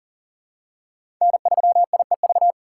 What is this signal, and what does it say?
Morse code, heard as a keyed tone, about 40 wpm, 700 Hz. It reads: N3IEV
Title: USA Call Signs - 5 Characters - 2x Character Spacing - 40wpm